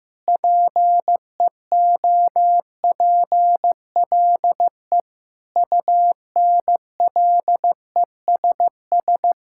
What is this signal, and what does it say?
Morse code, heard as a keyed tone, about 15 wpm, 700 Hz. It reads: PEOPLE UNLESS
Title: Sentences from Top 1000 Words - Review Part 6 - 15wpm